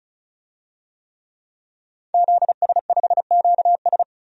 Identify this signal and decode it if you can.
Morse code, heard as a keyed tone, about 34 wpm, 700 Hz. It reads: ZS5QS
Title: DXCC-100 Call Signs - 5 Characters - 35wpm